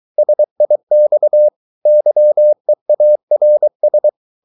Morse code, heard as a keyed tone, about 23 wpm, 600 Hz. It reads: SIX YEARS